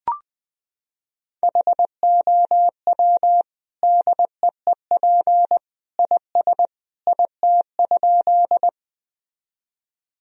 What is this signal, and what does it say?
Morse code, heard as a keyed tone, about 20 wpm, 700 Hz. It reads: HOW DEEP IS IT?
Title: Sentences from Top 1000 Words - Review Part 3 - 20wpm